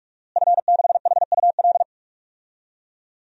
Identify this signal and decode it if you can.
Morse code, heard as a keyed tone, about 46 wpm, 700 Hz. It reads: F6HUL
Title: DXCC-100 Call Signs - 5 Characters - Repeated 2x - 45wpm